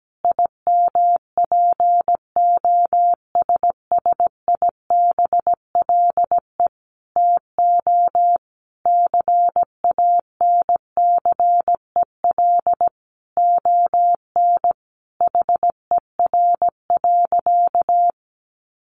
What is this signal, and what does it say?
Morse code, heard as a keyed tone, about 17 wpm, 700 Hz. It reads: IMPOSSIBLE TO CANCEL ON HER.
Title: Sentences from Top 1200 Words 17wpm